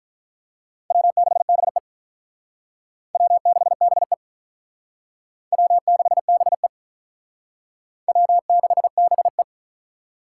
Morse code, heard as a keyed tone, about 45 wpm, 700 Hz. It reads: W6BE W6BE W6BE W6BE